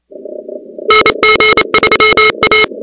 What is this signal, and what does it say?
Morse code, heard as a keyed tone, about 28 wpm, 2150 Hz, with noise. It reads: NG3A
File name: contests / 2019-cw-wpx-kt5j-k5pi- -2330 0-13